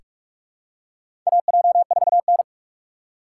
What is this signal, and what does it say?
Morse code, heard as a keyed tone, about 46 wpm, 690 Hz. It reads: AJ4N
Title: USA Call Signs - 4 Characters - 3x Character Spacing - 45wpm